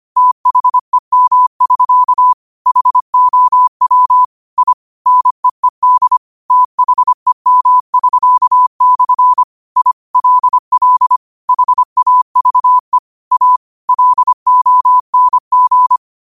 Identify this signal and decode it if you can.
Morse code, heard as a keyed tone, about 25 wpm, 1000 Hz. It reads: THEM<SK> HOW I NEED THEM<SK>/ I LL HAVE A LONG
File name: qotdfiles / 25wpm